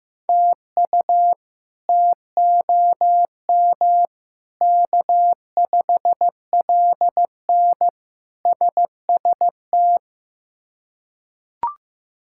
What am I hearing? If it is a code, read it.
Morse code, heard as a keyed tone, about 15 wpm, 700 Hz. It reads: TU TOM K5LN SST
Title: K1USN SST QSO Practice - 15wpm